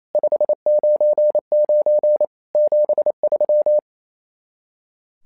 Morse code, heard as a keyed tone, about 28 wpm, 600 Hz. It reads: 599 73
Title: SOTA QSO Practice 28wpm